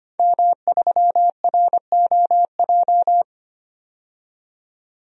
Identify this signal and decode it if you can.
Morse code, heard as a keyed tone, about 25 wpm, 700 Hz. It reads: M3ROJ
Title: DXCC-100 Call Signs - 5 Characters - 25wpm